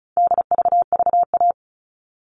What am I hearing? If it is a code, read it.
Morse code, heard as a keyed tone, about 34 wpm, 700 Hz. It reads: DVVA
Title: All Cut Numbers - 4-Digit 35wpm